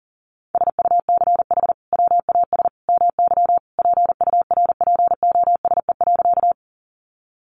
Morse code, heard as a keyed tone, about 40 wpm, 700 Hz. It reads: SUCH WAS MY PURPOSE.
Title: Sentences from Top 800 Words 40wpm